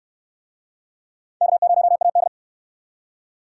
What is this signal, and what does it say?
Morse code, heard as a keyed tone, about 65 wpm, 700 Hz. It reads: K0AD